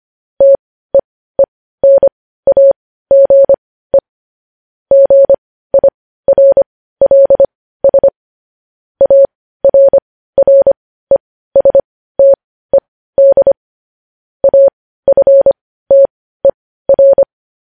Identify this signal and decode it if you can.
Morse code, heard as a keyed tone, about 25 wpm, spaced as for 15 wpm, 550 Hz. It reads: TEENAGE GIRLS ARRESTED AFTER
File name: podcast / News headlines-1x-25x15